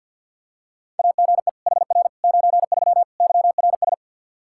Morse code, heard as a keyed tone, about 48 wpm, 700 Hz. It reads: AGE HR 93 YRS
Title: Practice QSO - Exchange 3 - More Info - 50wpm